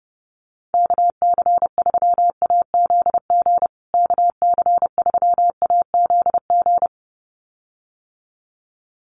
Morse code, heard as a keyed tone, about 30 wpm, 700 Hz. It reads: KC3AZG KC3AZG
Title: USA Call Signs Repeated 2x 30wpm